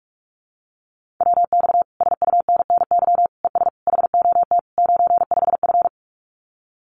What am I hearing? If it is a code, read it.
Morse code, heard as a keyed tone, about 46 wpm, 700 Hz. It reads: WX SUNNY ES HOT 95F
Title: Common QSO Phrases Encore 45wpm